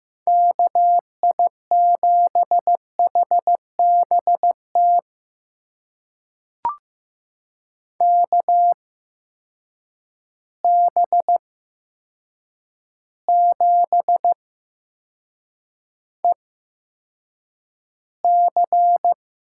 Morse code, heard as a keyed tone, about 15 wpm, spaced as for 7 wpm, 700 Hz. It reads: KI7HBT KB7EC
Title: USA Call Signs - 6 Characters - 8x Character Spacing - 15wpm